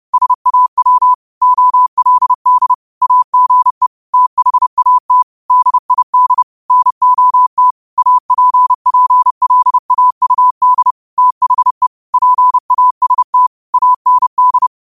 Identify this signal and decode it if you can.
Morse code, heard as a keyed tone, about 30 wpm, 1000 Hz. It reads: SAW OLD AGE THAT DID NOT APPLAUD THE PAST AND